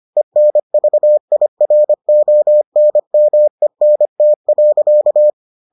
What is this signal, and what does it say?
Morse code, heard as a keyed tone, about 25 wpm, 600 Hz. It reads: ENVIRONMENT.